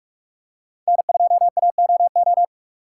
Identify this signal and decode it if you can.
Morse code, heard as a keyed tone, about 47 wpm, 700 Hz. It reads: N2AOO
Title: USA Call Signs - 6x Character Spacing - 45wpm